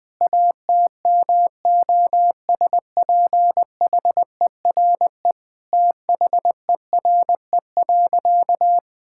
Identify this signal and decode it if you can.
Morse code, heard as a keyed tone, about 21 wpm, 700 Hz. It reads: ATMOSPHERE THERE.
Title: Sentences from Top 1600 Words 20wpm